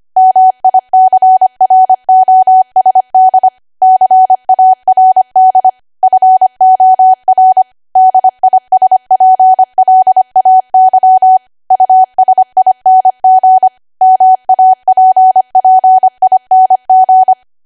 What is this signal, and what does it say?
Morse code, heard as a keyed tone, about 25 wpm, 750 Hz. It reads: MICROSD CARD FOR DISPLAY USING MAPPING
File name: Archive / 25 WPM / 25WPM